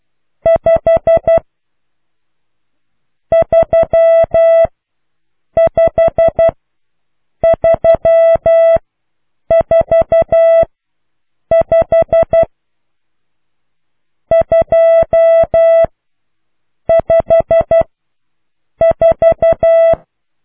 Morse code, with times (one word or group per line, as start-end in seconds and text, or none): 0.5-1.4 5
3.3-4.7 3
5.6-6.5 5
7.4-8.8 3
9.5-10.6 4
11.5-12.4 5
14.3-15.8 2
16.9-17.8 5
18.8-19.9 4